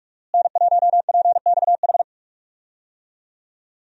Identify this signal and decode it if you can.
Morse code, heard as a keyed tone, about 45 wpm, 700 Hz. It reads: N1PXH
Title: USA Call Signs - 5 Characters - Repeated 3x - 45wpm